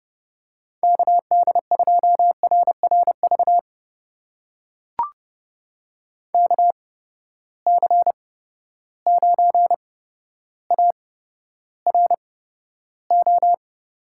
Morse code, heard as a keyed tone, about 30 wpm, spaced as for 13 wpm, 700 Hz. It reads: KD2RRV KC9ARO
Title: USA Call Signs - 6 Characters - 8x Character Spacing - 30wpm